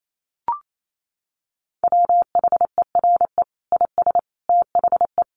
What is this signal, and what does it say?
Morse code, heard as a keyed tone, about 28 wpm, 700 Hz. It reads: WHERE IS THE
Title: Sentences from Top 1200 Words 28wpm